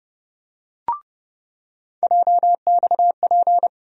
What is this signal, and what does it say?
Morse code, heard as a keyed tone, about 30 wpm, 700 Hz. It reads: JXP
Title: USA Call Sign Postfixes - 3 Letters 30wpm